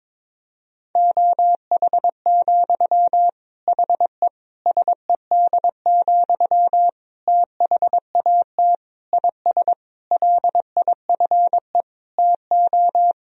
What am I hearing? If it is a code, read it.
Morse code, heard as a keyed tone, about 22 wpm, 700 Hz. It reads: OH, HE SED, THAT IS LIFE TO
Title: Sentences from Top 500 Words and Abbreviations 22wpm